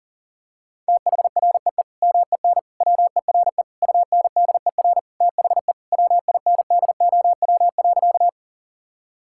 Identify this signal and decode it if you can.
Morse code, heard as a keyed tone, about 40 wpm, 700 Hz. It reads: THREE MEN WERE UNDER THE WINDOW.
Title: Sentences from Top 1600 Words 40wpm